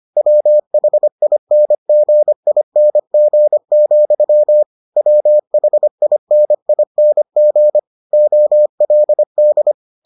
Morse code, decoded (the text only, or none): WHINGING, WHINING OLD